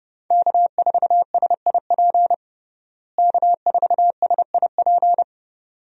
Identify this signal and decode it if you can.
Morse code, heard as a keyed tone, about 32 wpm, 700 Hz. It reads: K4SIP K4SIP
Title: USA Call Signs - 5 Characters - Repeated 2x - 30wpm